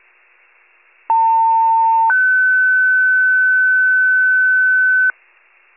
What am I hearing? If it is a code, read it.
Two-tone page, heard as a one-second tone, then a three-second tone, about 900 then 1500 Hz. It is heard from 1.1 to 5.1 s.